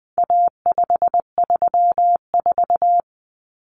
Morse code, heard as a keyed tone, about 20 wpm, 700 Hz. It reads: A534
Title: Common Cut Numbers - 4-Digit 20wpm